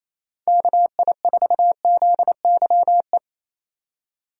Morse code, heard as a keyed tone, about 28 wpm, 700 Hz. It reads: KI4ZYE
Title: USA Call Signs - 6 Characters - 2x Character Spacing - 28wpm